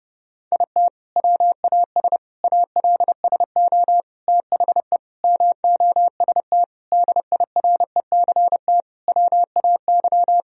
Morse code, heard as a keyed tone, about 30 wpm, 700 Hz. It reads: IT WAS ALSO THE MOST DIRECT WAY
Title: Sentences from Top 1000 Words - Review Part 2 - 30wpm